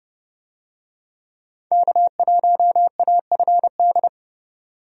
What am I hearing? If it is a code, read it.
Morse code, heard as a keyed tone, about 30 wpm, 700 Hz. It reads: K1AFD
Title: USA Call Signs - 5 Characters - 30wpm